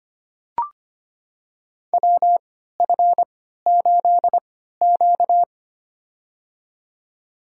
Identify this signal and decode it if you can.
Morse code, heard as a keyed tone, about 25 wpm, spaced as for 14 wpm, 700 Hz. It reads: WF8Q
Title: USA Call Signs - 4 Characters - 3x Character Spacing - 25wpm